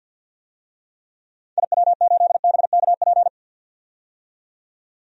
Morse code, heard as a keyed tone, about 48 wpm, 700 Hz. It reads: IW8BKP
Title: DXCC-100 Call Signs - 6 Characters - 50wpm